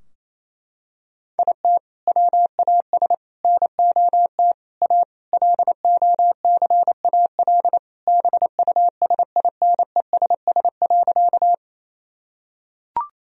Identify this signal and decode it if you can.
Morse code, heard as a keyed tone, about 27 wpm, 700 Hz. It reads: IT WAS NOT A LOCAL BUSINESS.
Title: Sentences from Top 1000 Words - Review Part 3 - 28wpm